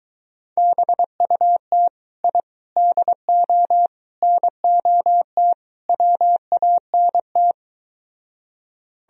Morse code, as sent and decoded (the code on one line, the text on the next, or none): -... ..- - / .. / -.. --- / -. --- - / .-- .- -. -
BUT I DO NOT WANT